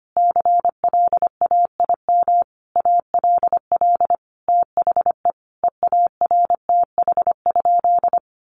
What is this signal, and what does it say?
Morse code, heard as a keyed tone, about 25 wpm, 700 Hz. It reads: CLAIM ALL THE EARTH?
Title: Sentences from Top 1500 Words 25wpm